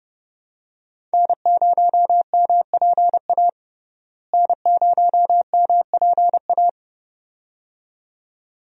Morse code, heard as a keyed tone, about 30 wpm, 700 Hz. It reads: N0MPA N0MPA
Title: USA Call Signs - 5 Characters - Repeated 2x - 30wpm